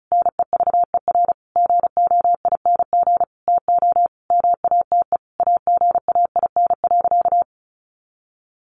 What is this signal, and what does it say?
Morse code, heard as a keyed tone, about 34 wpm, 700 Hz. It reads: NEVER GOING TO MATE AGAIN.
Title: A Sound Of Thunder 35wpm